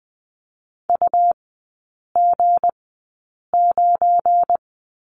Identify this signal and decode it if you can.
Morse code, heard as a keyed tone, about 20 wpm, spaced as for 8 wpm, 690 Hz. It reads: UG9
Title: 212 Lesson 34e - Number 8 - Three Characters 20wpm